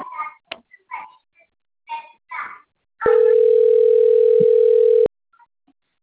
Ringback tone (North America), 440 + 480 Hz.